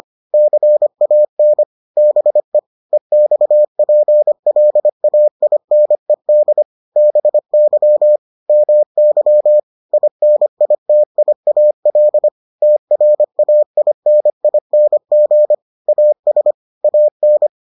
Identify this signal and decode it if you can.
Morse code, heard as a keyed tone, about 25 wpm, 600 Hz. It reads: CAN BE EXPLAINED BY MY INITIAL TRAINING AS AN